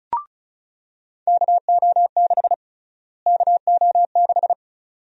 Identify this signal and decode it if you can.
Morse code, heard as a keyed tone, about 34 wpm, 700 Hz. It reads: KO6 KO6
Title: USA Call Sign Prefixes - Repeated 2x - 35wpm